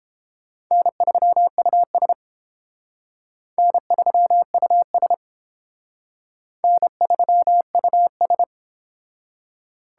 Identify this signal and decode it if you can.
Morse code, heard as a keyed tone, about 30 wpm, 700 Hz. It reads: N3US N3US N3US